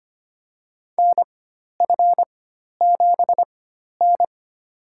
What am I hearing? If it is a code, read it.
Morse code, heard as a keyed tone, about 25 wpm, spaced as for 12 wpm, 700 Hz. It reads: NF7N
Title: USA Call Signs - 4 Characters - 4x Character Spacing - 25wpm